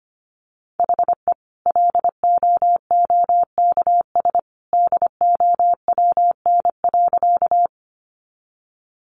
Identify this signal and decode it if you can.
Morse code, heard as a keyed tone, about 25 wpm, 700 Hz. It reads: HE LOOKS DOWN.
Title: Additional Easy Phrases 25wpm